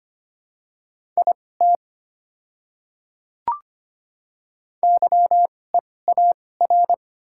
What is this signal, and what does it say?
Morse code, heard as a keyed tone, about 25 wpm, spaced as for 18 wpm, 690 Hz. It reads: IT YEAR
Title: 2x Character Spacing - Top 200 Words - 25wpm